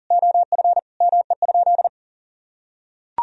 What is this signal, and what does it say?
Morse code, heard as a keyed tone, about 40 wpm, 700 Hz. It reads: OF ME?
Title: Sentences from Top 1000 Words - Review Part 4 - 40wpm